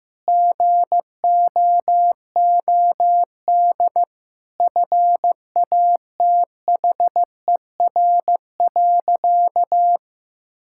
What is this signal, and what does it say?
Morse code, heard as a keyed tone, about 15 wpm, 700 Hz. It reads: GOOD FATHER.
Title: Sentences from Top 1000 Words - Review Part 5 - 15wpm